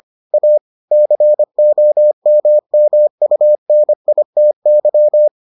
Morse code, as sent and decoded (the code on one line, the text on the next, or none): .- / -.-. --- -- -- ..- -. .. - -.--
A COMMUNITY